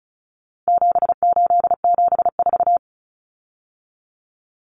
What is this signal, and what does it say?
Morse code, heard as a keyed tone, about 34 wpm, 700 Hz. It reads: 7874
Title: Common Cut Numbers - 4-Digit 35wpm